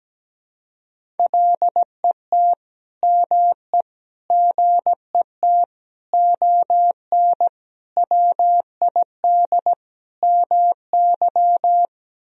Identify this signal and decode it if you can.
Morse code, heard as a keyed tone, about 17 wpm, 700 Hz. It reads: LET ME GET ON WID MY
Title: Sentences from Top 300 Words and Abbreviations 17wpm